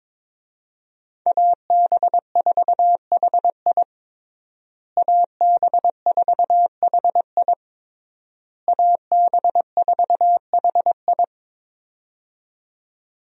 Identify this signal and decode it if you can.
Morse code, heard as a keyed tone, about 22 wpm, 700 Hz. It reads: AB4HI AB4HI AB4HI
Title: USA Call Signs - 5 Characters - Repeated 3x - 22wpm